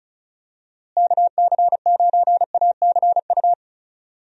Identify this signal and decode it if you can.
Morse code, heard as a keyed tone, about 36 wpm, 700 Hz. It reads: KC9ACU
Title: USA Call Signs - 6 Characters - Repeated 3x - 35wpm